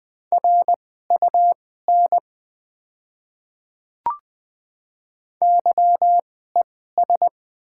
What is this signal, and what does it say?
Morse code, heard as a keyed tone, about 20 wpm, spaced as for 15 wpm, 690 Hz. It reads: RUN YES
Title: 2x Character Spacing - Top 500 Words - 20wpm